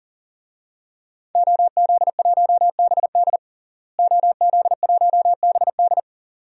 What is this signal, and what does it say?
Morse code, heard as a keyed tone, about 40 wpm, 700 Hz. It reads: OZ1BD OZ1BD
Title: DXCC-100 Call Signs - 5 Characters - Repeated 2x - 40wpm